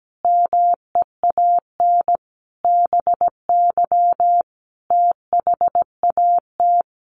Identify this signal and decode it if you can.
Morse code, heard as a keyed tone, about 17 wpm, 700 Hz. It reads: MEAN BY THAT
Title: Sentences from Top 400 Words 17wpm